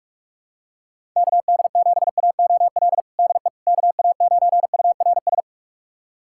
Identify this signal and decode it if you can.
Morse code, heard as a keyed tone, about 46 wpm, 700 Hz. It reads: KD7AOL DE KA9URS
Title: USA Callsign DE Callsign 45wpm